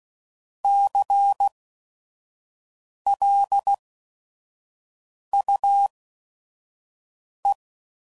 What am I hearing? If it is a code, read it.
Morse code, heard as a keyed tone, about 16 wpm, spaced as for 5 wpm, 800 Hz. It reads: CLUE
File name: Prefixes / By First Letter / Cs